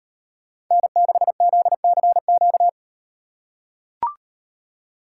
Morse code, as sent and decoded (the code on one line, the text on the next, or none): -. -.... --.. -.-. --.-
N6ZCQ